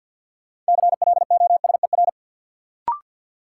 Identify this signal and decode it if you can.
Morse code, heard as a keyed tone, about 51 wpm, 700 Hz. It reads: CLOSER
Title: Sets of 5 Words Encore - New Word Drill - 50wpm